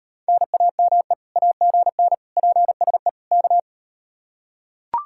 Morse code, heard as a keyed tone, about 39 wpm, 700 Hz. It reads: NAME AGN PSE K